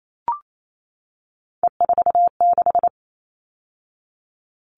Morse code, heard as a keyed tone, about 28 wpm, 700 Hz. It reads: E46
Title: All Cut Numbers - 3-Digit 28wpm